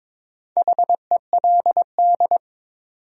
Morse code, heard as a keyed tone, about 22 wpm, 700 Hz. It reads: HELD
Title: Sentences from Top 1600 Words - New Word Drill - 22wpm